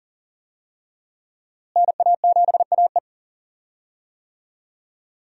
Morse code, heard as a keyed tone, about 40 wpm, 700 Hz. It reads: NA7AE
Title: USA Call Signs - 5 Characters - Repeated 2x - 40wpm